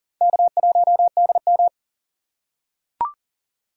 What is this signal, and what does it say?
Morse code, heard as a keyed tone, about 39 wpm, 700 Hz. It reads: K1DM
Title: K1USN SST QSO Practice - 40wpm